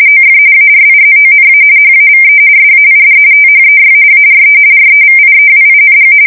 RTTY, about 45.45 baud. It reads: OF ARIZONA HAVE SAID THEY ARE REVIEWING
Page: 170 Shift 2125/2295